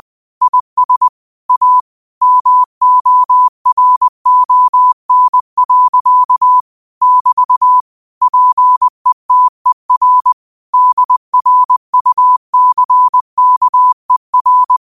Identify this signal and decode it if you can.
Morse code, heard as a keyed tone, about 20 wpm, 1000 Hz. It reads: IS A MORON. <BT> PETER DRUCKER
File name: qotdfiles / 20wpm